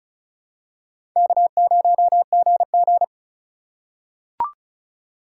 Morse code, heard as a keyed tone, about 36 wpm, 700 Hz. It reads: K0GG
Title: USA Call Signs - 4 Characters - 4x Character Spacing - 35wpm